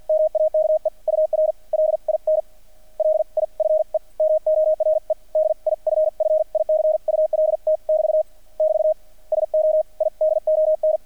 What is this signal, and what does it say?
Morse code, heard as a keyed tone, about 50 wpm, spaced as for 39 wpm, 620 Hz, with radio noise. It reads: MAKE UR LINLIFE MORE DIFFICULT <BT> <BT> SO I DON
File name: archived / CRSnet